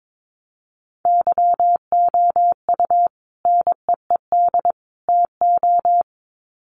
Morse code, heard as a keyed tone, about 22 wpm, 700 Hz. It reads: YOU NEED TO
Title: Sets of 3 Words 22wpm